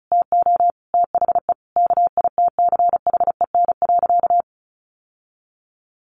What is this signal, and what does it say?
Morse code, heard as a keyed tone, about 34 wpm, 700 Hz. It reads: TO THE KITCHEN.
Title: Sentences from Top 1500 Words 35wpm